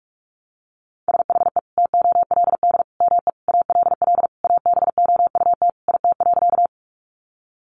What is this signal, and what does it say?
Morse code, heard as a keyed tone, about 44 wpm, 700 Hz. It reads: SHE TOLD ME ALL ABOUT IT.
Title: Sentences from Top 400 Words 45wpm